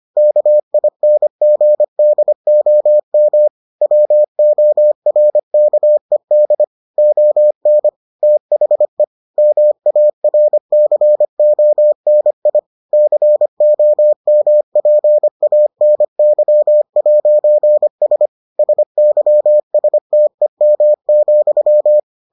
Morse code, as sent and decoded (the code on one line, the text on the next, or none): -.- .. -. --. -.. --- -- / .-- --- .-. -.- . -.. / --- -. / - .... / -- .- .-. -.-. --- -. .. / -.-. --- -- .--. .- -. -.-- .----. ... / ... -.-- ... - . -- --..--
KINGDOM WORKED ON THE MARCONI COMPANY'S SYSTEM,